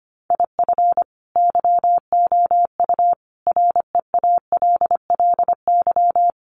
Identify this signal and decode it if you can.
Morse code, heard as a keyed tone, about 25 wpm, 700 Hz. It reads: IF YOU REALLY
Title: Sets of 5 Words Encore2 25wpm